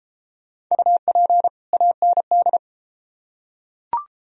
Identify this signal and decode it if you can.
Morse code, heard as a keyed tone, about 29 wpm, 700 Hz. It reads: UP AND